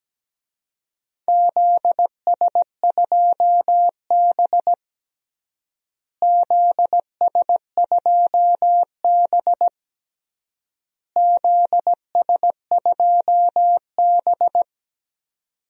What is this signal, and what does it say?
Morse code, heard as a keyed tone, about 17 wpm, 700 Hz. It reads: ZS2B ZS2B ZS2B
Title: DXCC-100 Call Signs - 4 Characters - Repeated 3x - 17wpm